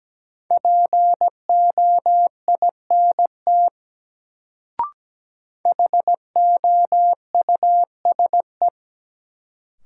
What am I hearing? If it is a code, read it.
Morse code, heard as a keyed tone, about 17 wpm, 700 Hz. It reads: POINT HOUSE
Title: Sentences from Top 200 Words - New Word Drill - 17wpm